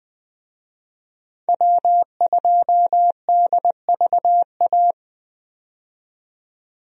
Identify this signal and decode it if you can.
Morse code, heard as a keyed tone, about 21 wpm, 700 Hz. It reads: W2DVA
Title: USA Call Signs - 5 Characters - 20wpm